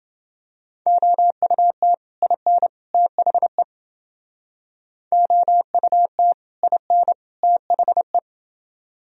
Morse code, heard as a keyed tone, about 28 wpm, 700 Hz. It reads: OUT IN THE OUT IN THE